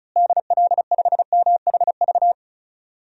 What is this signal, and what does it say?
Morse code, heard as a keyed tone, about 36 wpm, 700 Hz. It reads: DL5MHV
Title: DXCC-100 Call Signs - 6 Characters - Repeated 2x - 35wpm